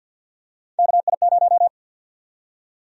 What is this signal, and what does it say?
Morse code, heard as a keyed tone, about 51 wpm, 700 Hz. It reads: KI0
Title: USA Call Sign Prefixes - Repeated 2x - 50wpm